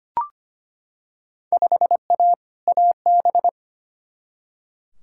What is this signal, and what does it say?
Morse code, heard as a keyed tone, about 25 wpm, 700 Hz. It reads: 5A AB
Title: Field Day Exchanges Encore - 25wpm